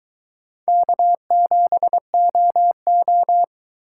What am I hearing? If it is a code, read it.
Morse code, heard as a keyed tone, about 22 wpm, 700 Hz. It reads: K7OO